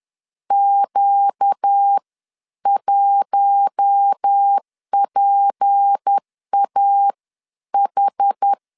Morse code, heard as a keyed tone, about 10 wpm, spaced as for 9 wpm, 800 Hz. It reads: Q1PAH
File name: LEZIONI / 6 PKX0 / 6 P 10wpm 2432